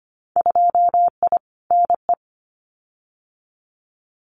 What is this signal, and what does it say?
Morse code, heard as a keyed tone, about 25 wpm, 700 Hz. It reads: 2I NE
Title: Winter Field Day Exchanges Encore 25wpm